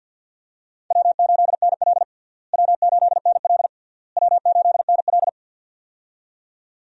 Morse code, heard as a keyed tone, about 48 wpm, 700 Hz. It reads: W8NL W8NL W8NL
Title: USA Call Signs - 4 Characters - Repeated 3x - 50wpm